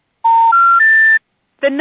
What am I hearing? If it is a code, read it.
Special information tone, with noise.
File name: 757-854-1100-Verizon-generic-centralized-intercept